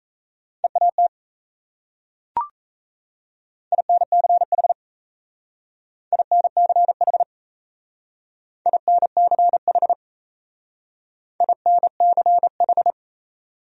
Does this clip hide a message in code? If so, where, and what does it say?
Yes: Morse code from 0.6 to 12.9 s, heard as a keyed tone, about 36 wpm, 700 Hz. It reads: EAT INCH INCH INCH INCH